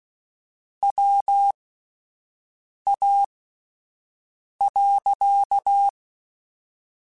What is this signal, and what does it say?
Morse code, heard as a keyed tone, about 16 wpm, spaced as for 5 wpm, 800 Hz. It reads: WA.
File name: QSOs Spaced